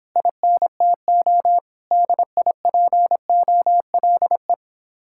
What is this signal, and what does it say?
Morse code, heard as a keyed tone, about 26 wpm, 700 Hz. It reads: INTO DIPOLE